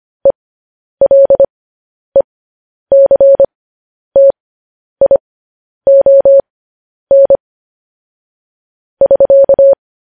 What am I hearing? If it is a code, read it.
Morse code, heard as a keyed tone, about 25 wpm, spaced as for 10 wpm, 550 Hz. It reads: ELECTION <SK>
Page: news headlines in morse code at 10 wpm